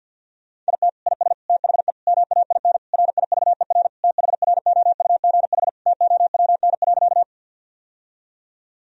Morse code, heard as a keyed tone, about 51 wpm, 700 Hz. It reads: IT IS THE MAIN RIVER THROUGH TOWN.
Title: Sentences from Top 1000 Words - Review Part 2 - 50wpm